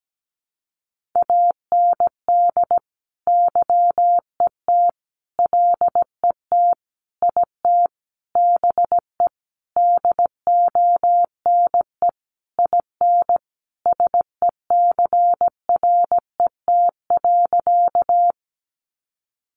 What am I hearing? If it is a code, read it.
Morse code, heard as a keyed tone, about 17 wpm, 700 Hz. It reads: AND YET LET IT BE DONE IN SECRET.
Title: Sentences from Top 1200 Words 17wpm